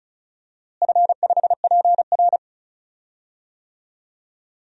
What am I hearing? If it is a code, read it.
Morse code, heard as a keyed tone, about 36 wpm, 700 Hz. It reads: F5PR
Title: DXCC-100 Call Signs - 4 Characters - Repeated 3x - 35wpm